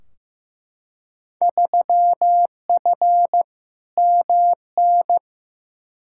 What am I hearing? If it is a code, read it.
Morse code, heard as a keyed tone, about 15 wpm, 700 Hz. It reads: 3F MN
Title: Field Day Exchanges Encore - 15wpm